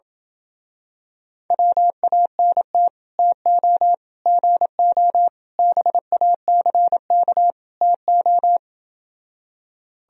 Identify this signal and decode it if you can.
Morse code, heard as a keyed tone, about 27 wpm, 700 Hz. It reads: WANT TO GO BACK TO